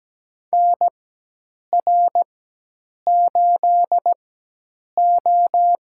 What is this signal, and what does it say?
Morse code, heard as a keyed tone, about 17 wpm, spaced as for 8 wpm, 700 Hz. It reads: NR8O
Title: USA Call Signs - 4 Characters - 4x Character Spacing - 17wpm